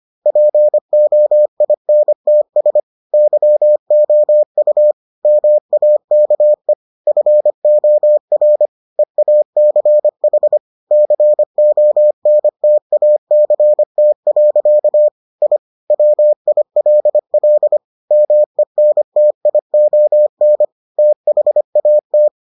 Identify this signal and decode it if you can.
Morse code, heard as a keyed tone, about 25 wpm, 600 Hz. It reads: POINTS YOU MAKE FOR EACH CONTACT. I WILL MENTION THAT